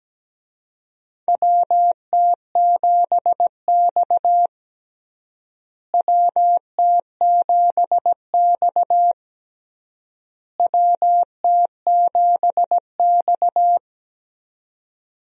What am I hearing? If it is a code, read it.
Morse code, heard as a keyed tone, about 17 wpm, 700 Hz. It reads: WT7X WT7X WT7X
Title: USA Call Signs - 4 Characters - Repeated 3x - 17wpm